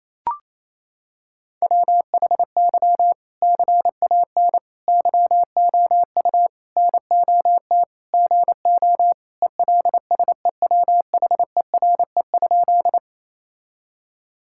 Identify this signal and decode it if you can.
Morse code, heard as a keyed tone, about 28 wpm, 700 Hz. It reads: WHY CAN YOU NOT GO ELSEWHERE?
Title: Sentences from Top 1600 Words 28wpm